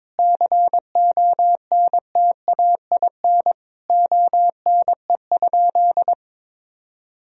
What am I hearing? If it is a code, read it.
Morse code, heard as a keyed tone, about 22 wpm, 700 Hz. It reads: CONTAIN ONE?
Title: Sentences from Top 1000 Words - Review Part 2 - 22wpm